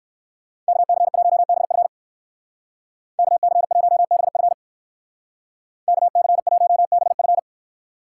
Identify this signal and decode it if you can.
Morse code, heard as a keyed tone, about 59 wpm, 700 Hz. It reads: KC1BF KC1BF KC1BF